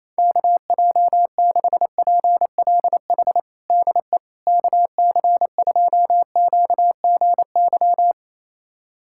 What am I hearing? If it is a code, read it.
Morse code, heard as a keyed tone, about 28 wpm, 700 Hz. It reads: KJ6PLH DE KC2QGY
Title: USA Callsign DE Callsign - 6 Characters - 28wpm